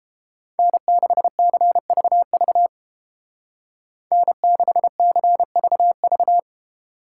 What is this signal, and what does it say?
Morse code, heard as a keyed tone, about 29 wpm, 700 Hz. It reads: N6CVV N6CVV